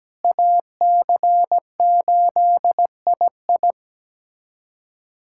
Morse code, heard as a keyed tone, about 17 wpm, 700 Hz. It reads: AC8II
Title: USA Callsign DE Callsign - 5 Characters - 17wpm